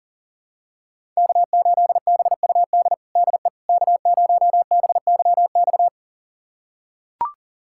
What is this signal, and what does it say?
Morse code, heard as a keyed tone, about 40 wpm, 700 Hz. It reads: K8BUD DE K0BYX
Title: USA Callsign DE Callsign - 5 Characters - 40wpm